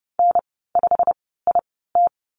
Morse code, heard as a keyed tone, about 30 wpm, spaced as for 17 wpm, 700 Hz. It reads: N5IT
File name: Lesson 15g - Number 5 - Call Signs 30wpm